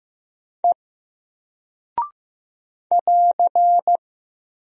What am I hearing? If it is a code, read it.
Morse code, heard as a keyed tone, about 15 wpm, 700 Hz. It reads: E <AR>
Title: Single Character (Letter:Number:Punctuation:Prosign) Alt 15wpm